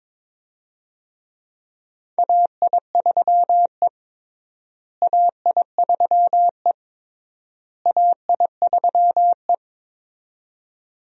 Morse code, heard as a keyed tone, about 22 wpm, 700 Hz. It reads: AI3E AI3E AI3E
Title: USA Call Signs - 4 Characters - Repeated 3x - 22wpm